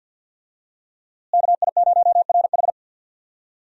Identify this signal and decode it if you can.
Morse code, heard as a keyed tone, about 50 wpm, 700 Hz. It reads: KI0RH
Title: USA Call Signs - 5 Characters - Repeated 2x - 50wpm